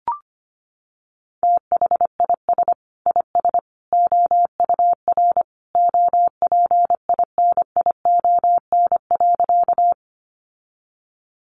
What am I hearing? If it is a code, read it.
Morse code, heard as a keyed tone, about 25 wpm, 700 Hz. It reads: THIS IS OUR OPINION.
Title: Sentences from Top 1200 Words 25wpm